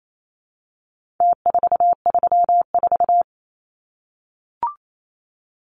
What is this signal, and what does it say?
Morse code, heard as a keyed tone, about 28 wpm, 700 Hz. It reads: T434
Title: Numbers - Contest Style - 4-Digit 28wpm